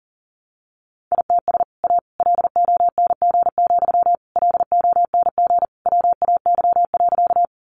Morse code, heard as a keyed tone, about 40 wpm, 700 Hz. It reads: ITS A LONG, LONG WAY.
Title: Sentences from Top 600 Words 40wpm